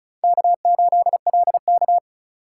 Morse code, heard as a keyed tone, about 34 wpm, 700 Hz. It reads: K8LK
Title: USA Call Signs - 4 Characters - 4x Character Spacing - 35wpm